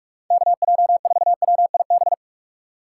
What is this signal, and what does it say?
Morse code, heard as a keyed tone, about 59 wpm, 700 Hz. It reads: KJ4WIB